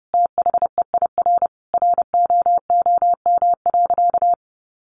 Morse code, heard as a keyed tone, about 30 wpm, 700 Hz. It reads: THEIR ROOM.
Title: Easy Phrases 30wpm